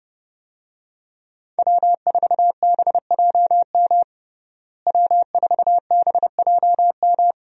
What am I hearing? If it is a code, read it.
Morse code, heard as a keyed tone, about 32 wpm, 700 Hz. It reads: W4BJM W4BJM
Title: USA Call Signs - 5 Characters - Repeated 3x - 30wpm